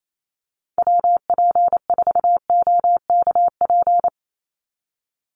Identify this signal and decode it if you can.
Morse code, heard as a keyed tone, about 28 wpm, 700 Hz. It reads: WP4OKP
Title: USA Call Signs 28wpm